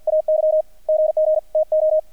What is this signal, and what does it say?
Morse code, heard as a keyed tone, about 48 wpm, 620 Hz, with radio noise. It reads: AY GG TO
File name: archived / CRSnet